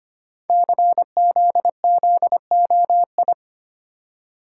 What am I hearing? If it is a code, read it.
Morse code, heard as a keyed tone, about 25 wpm, 700 Hz. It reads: CZZOI
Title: Code Groups - 5 Letters - 25wpm